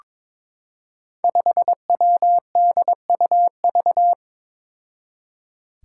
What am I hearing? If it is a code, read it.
Morse code, heard as a keyed tone, about 22 wpm, 700 Hz. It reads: 5WDUV
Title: Five Challenging Letter-Number Combinations 22wpm